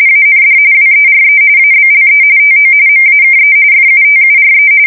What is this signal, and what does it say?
RTTY, about 45.45 baud. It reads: AND WERE ABLE TO MAKE FUTURE
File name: fsk170 / USS-New-Jersey